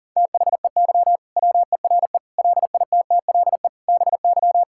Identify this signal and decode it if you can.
Morse code, heard as a keyed tone, about 40 wpm, 700 Hz. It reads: THEY WERE LITTLE BY
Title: Sentences from Top 1600 Words 40wpm